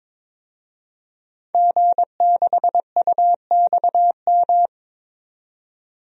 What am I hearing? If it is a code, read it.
Morse code, heard as a keyed tone, about 22 wpm, 700 Hz. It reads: G6UXM
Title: DXCC-100 Call Signs - 5 Characters - 22wpm